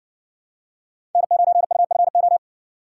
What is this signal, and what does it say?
Morse code, heard as a keyed tone, about 60 wpm, 700 Hz. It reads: N9UFO